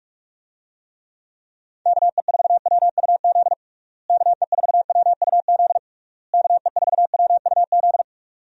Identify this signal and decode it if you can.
Morse code, heard as a keyed tone, about 46 wpm, 700 Hz. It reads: KE4WUZ KE4WUZ KE4WUZ
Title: USA Call Signs - 6 Characters - Repeated 3x - 45wpm